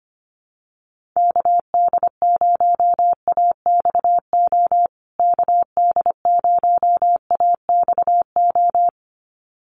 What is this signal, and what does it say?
Morse code, heard as a keyed tone, about 25 wpm, 700 Hz. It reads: KD0AXO KD0AXO
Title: USA Call Signs Repeated 2x 25wpm